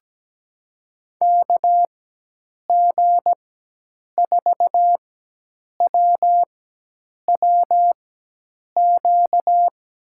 Morse code, heard as a keyed tone, about 17 wpm, spaced as for 8 wpm, 700 Hz. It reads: KG4WWQ
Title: USA Call Signs - 6 Characters - 4x Character Spacing - 17wpm